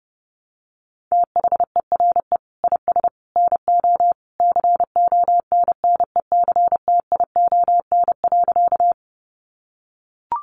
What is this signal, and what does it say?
Morse code, heard as a keyed tone, about 30 wpm, 700 Hz. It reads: THERE IS NO CONNECTION.
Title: Sentences from Top 1300 Words 30wpm